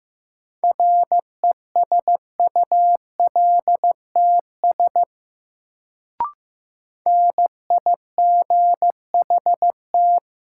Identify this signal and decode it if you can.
Morse code, heard as a keyed tone, about 15 wpm, 700 Hz. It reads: RESULTS NIGHT
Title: Sets of 4 Words - New Word Drill - 15wpm